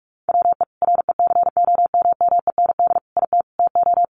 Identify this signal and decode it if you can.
Morse code, heard as a keyed tone, about 44 wpm, 700 Hz. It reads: WE RECOMMEND IT TO
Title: Sentences from Top 1200 Words 45wpm